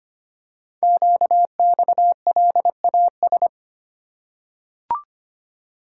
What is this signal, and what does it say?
Morse code, heard as a keyed tone, about 25 wpm, 700 Hz. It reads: QXLAS
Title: Code Groups - 5 Letters - 25wpm